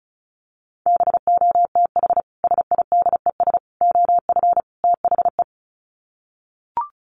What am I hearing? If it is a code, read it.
Morse code, heard as a keyed tone, about 34 wpm, 700 Hz. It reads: BOTH SIDES OF THE
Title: Sets of 4 Words 35wpm